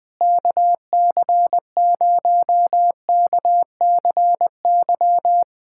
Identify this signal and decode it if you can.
Morse code, heard as a keyed tone, about 21 wpm, 700 Hz. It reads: KC0KCY
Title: USA Callsign DE Callsign 20wpm